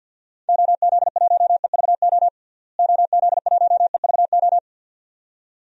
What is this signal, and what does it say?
Morse code, heard as a keyed tone, about 49 wpm, 700 Hz. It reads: OZ1EVO OZ1EVO
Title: DXCC-100 Call Signs - 6 Characters - Repeated 2x - 50wpm